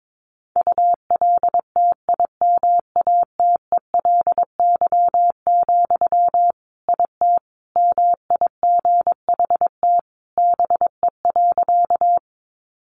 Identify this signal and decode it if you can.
Morse code, heard as a keyed tone, about 22 wpm, 700 Hz. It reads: ULTIMATELY, IT MIGHT BE.
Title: Sentences from Top 1200 Words 22wpm